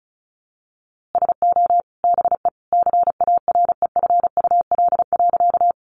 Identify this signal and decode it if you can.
Morse code, heard as a keyed tone, about 34 wpm, 700 Hz. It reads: SO BE CAREFUL.
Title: A Sound Of Thunder 35wpm